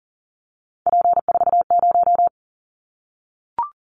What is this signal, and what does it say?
Morse code, heard as a keyed tone, about 40 wpm, 690 Hz. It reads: P40
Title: Callsign Prefixes for Top 100 DXCC - 40wpm